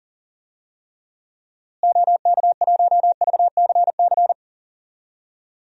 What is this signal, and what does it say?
Morse code, heard as a keyed tone, about 37 wpm, 700 Hz. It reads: OK1VCC
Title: DXCC-100 Call Signs - 6 Characters - Repeated 2x - 40wpm